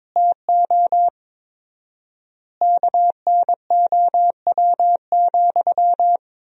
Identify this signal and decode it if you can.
Morse code, heard as a keyed tone, about 22 wpm, 700 Hz. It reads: TO KNOW,
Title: 4x Word-Spacing - Sentences from Top 200 Words - 22wpm